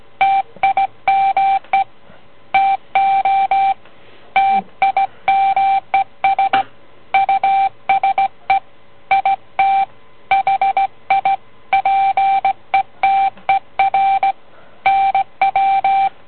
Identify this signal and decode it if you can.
Morse code, heard as a keyed tone, about 17 wpm, 780 Hz. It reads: TIME TO TIMES USE IT HI PETER NW